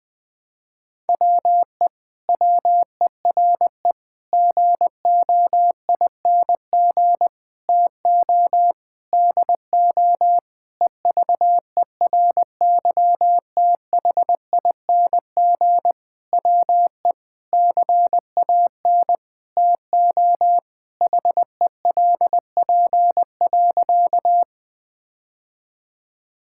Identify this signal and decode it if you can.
Morse code, heard as a keyed tone, about 20 wpm, 700 Hz. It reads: WE WERE GOING TO DO EVERYTHING WE CAN TO HELP.
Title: Sentences with 8-Word Phrases - 20wpm